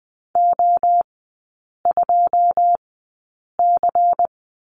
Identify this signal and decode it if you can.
Morse code, heard as a keyed tone, about 20 wpm, spaced as for 8 wpm, 700 Hz. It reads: O2C
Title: 093 Lesson 16e - Letter c - Three Characters 20wpm